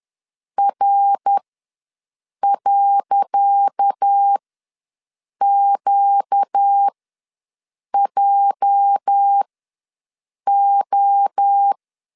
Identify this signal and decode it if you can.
Morse code, heard as a keyed tone, about 10 wpm, spaced as for 6 wpm, 800 Hz. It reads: R.QJO
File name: LEZIONI / 10 SIGNS / 10C DOT ws 10wpm 4213